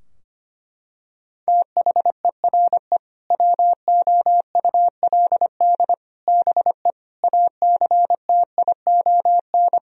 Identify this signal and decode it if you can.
Morse code, heard as a keyed tone, about 25 wpm, 700 Hz. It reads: THERE WOULD BE ACTION
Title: Sentences from Top 1000 Words - Review Part 4 - 25wpm